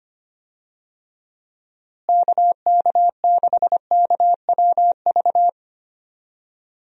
Morse code, heard as a keyed tone, about 25 wpm, 700 Hz. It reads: KK6KWV
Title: USA Call Signs - 6 Characters - Repeated 3x - 25wpm